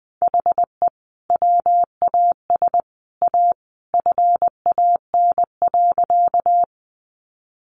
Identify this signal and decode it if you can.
Morse code, heard as a keyed tone, about 20 wpm, 700 Hz. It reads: HE WAS A FAN.
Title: Sentences from Top 1300 Words 20wpm